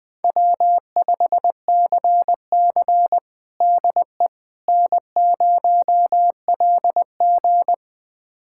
Morse code, heard as a keyed tone, about 21 wpm, 700 Hz. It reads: W5CC DE N0LG
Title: USA Callsign DE Callsign - 4 Characters - 20wpm